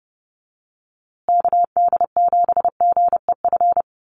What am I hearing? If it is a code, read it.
Morse code, heard as a keyed tone, about 30 wpm, 700 Hz. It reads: KD7GEF
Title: USA Call Signs Repeated 3x 30wpm